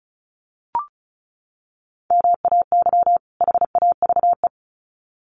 Morse code, heard as a keyed tone, about 34 wpm, 700 Hz. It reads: MAY HAVE
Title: Sets of 2 Words 35wpm